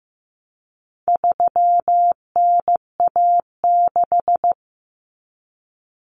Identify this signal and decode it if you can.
Morse code, heard as a keyed tone, about 15 wpm, 700 Hz. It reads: 3NA6
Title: Common Cut Numbers - 4-Digit 15wpm